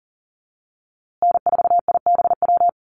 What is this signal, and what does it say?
Morse code, heard as a keyed tone, about 39 wpm, 700 Hz. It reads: N4IBW
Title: USA Call Signs Repeated 3x 40wpm